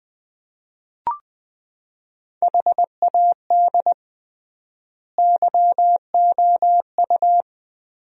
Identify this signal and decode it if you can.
Morse code, heard as a keyed tone, about 20 wpm, 700 Hz. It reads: HAD YOU
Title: 3x Word-Spacing - Sentences from Top 400 Words - 20wpm